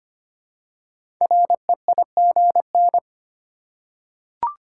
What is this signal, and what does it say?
Morse code, heard as a keyed tone, about 25 wpm, 700 Hz. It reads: REIGN
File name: 5 Letter Words - First 3 Letters Given - 25wpm